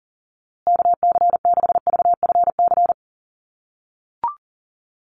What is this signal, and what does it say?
Morse code, heard as a keyed tone, about 40 wpm, 700 Hz. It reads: KC6VFC
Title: USA Call Signs Repeated 2x 40wpm